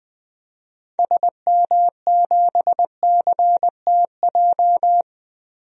Morse code, heard as a keyed tone, about 20 wpm, 700 Hz. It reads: SM7CTJ
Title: DXCC-100 Call Signs - 6 Characters - Repeated 2x - 20wpm